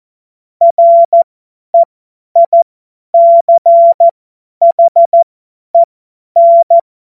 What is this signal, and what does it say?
Morse code, heard as a keyed tone, about 14 wpm, spaced as for 10 wpm, 680 Hz. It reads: REICHEN
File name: Morsekurs / Witze / Witz 17. Mai 2022 50 F70